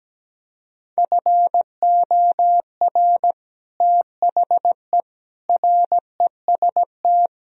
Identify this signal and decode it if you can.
Morse code, heard as a keyed tone, about 17 wpm, 700 Hz. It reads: FOR THE REST
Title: Speed-Racing - Sets of 4 Words 17wpm